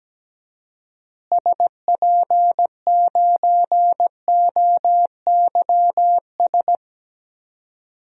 Morse code, heard as a keyed tone, about 17 wpm, 700 Hz. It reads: SP9OYS
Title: DXCC-100 Call Signs - 6 Characters - Repeated 2x - 17wpm